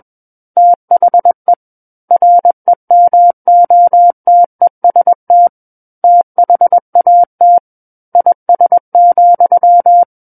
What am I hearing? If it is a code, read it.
Morse code, heard as a keyed tone, about 21 wpm, 700 Hz. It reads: THE REMOTEST THAT IS,
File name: Book One / 12-Chapter 12 - 21 WPM